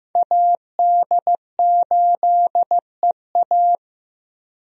Morse code, heard as a keyed tone, about 15 wpm, 700 Hz. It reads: AD8EA
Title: USA Call Signs - 5 Characters - Repeated 3x - 15wpm